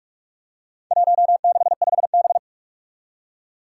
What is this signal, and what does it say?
Morse code, heard as a keyed tone, about 46 wpm, 700 Hz. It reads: 165B
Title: Four Challenging Letter-Number Combinations 45wpm